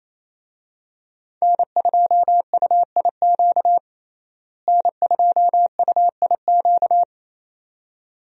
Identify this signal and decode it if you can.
Morse code, heard as a keyed tone, about 28 wpm, 700 Hz. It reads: N2UIQ N2UIQ
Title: USA Call Signs - 5 Characters - Repeated 2x - 28wpm